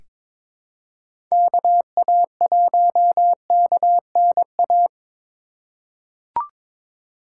Morse code, heard as a keyed tone, about 22 wpm, 700 Hz. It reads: KA1KNA
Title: USA Call Signs - 6 Characters - 3x Character Spacing - 22wpm